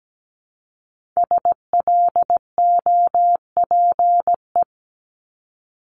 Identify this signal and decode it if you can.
Morse code, heard as a keyed tone, about 17 wpm, 700 Hz. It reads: SLOPE
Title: Multi-letter Phonemes Repeated 3x 17wpm